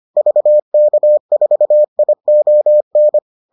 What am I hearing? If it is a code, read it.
Morse code, heard as a keyed tone, about 25 wpm, 600 Hz. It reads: VK4ION